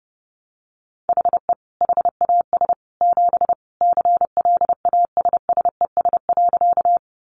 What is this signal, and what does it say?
Morse code, heard as a keyed tone, about 30 wpm, 700 Hz. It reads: HE HAS 7 CLASSES.
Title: Additional Easy Phrases 30wpm